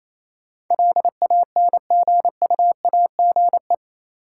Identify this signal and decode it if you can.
Morse code, heard as a keyed tone, about 27 wpm, 700 Hz. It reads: LANGUAGE
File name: Easy Words - Speed-Racing with Hints - 22wpm